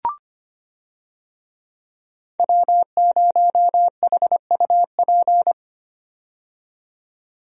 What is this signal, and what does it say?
Morse code, heard as a keyed tone, about 25 wpm, 700 Hz. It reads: W0HUP
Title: USA Call Signs - 5 Characters - 25wpm